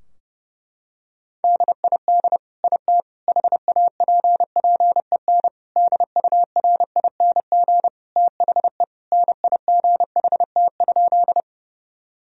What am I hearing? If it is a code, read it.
Morse code, heard as a keyed tone, about 32 wpm, 700 Hz. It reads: DID IT HAPPEN DURING THE NIGHT?
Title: Sentences from Top 1000 Words - Review Part 2 - 30wpm